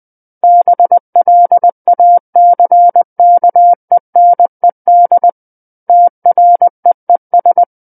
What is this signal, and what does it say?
Morse code, heard as a keyed tone, about 20 wpm, 700 Hz. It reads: BLACKENED TREES
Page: SKCC Learning Center - Morse Code Practice Files